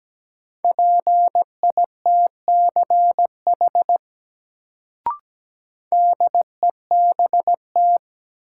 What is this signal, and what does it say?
Morse code, heard as a keyed tone, about 17 wpm, 700 Hz. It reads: PITCH DEBT
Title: Sentences from Top 1300 Words - New Word Drill Alt - 17wpm